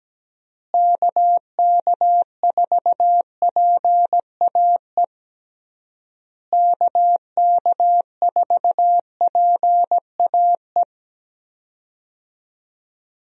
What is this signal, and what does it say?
Morse code, heard as a keyed tone, about 17 wpm, 700 Hz. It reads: KK4PAE KK4PAE
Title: USA Call Signs - 6 Characters - Repeated 3x - 17wpm